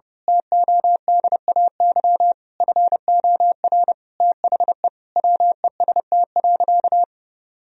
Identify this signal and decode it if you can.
Morse code, heard as a keyed tone, about 30 wpm, 700 Hz. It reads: TODAY FOR THE WEST.
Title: Sentences from Top 1000 Words - Review Part 2 - 30wpm